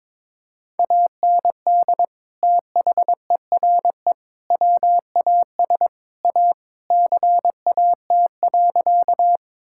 Morse code, heard as a keyed tone, about 22 wpm, 700 Hz. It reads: AND THERE WAS A CAT.
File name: practice / 22wpm / Sentences with 4-Word Phrases - 22wpm